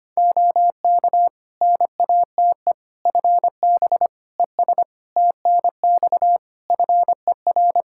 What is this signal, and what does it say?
Morse code, heard as a keyed tone, about 25 wpm, 700 Hz. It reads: OK NATE FB ES TNX FER
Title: Practice QSO - Exchange 2 - Info - 25wpm